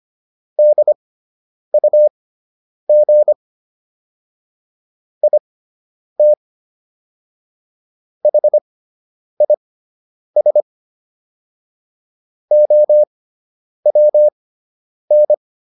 Morse code, heard as a keyed tone, about 25 wpm, spaced as for 9 wpm, 600 Hz. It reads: DUG IT HIS OWN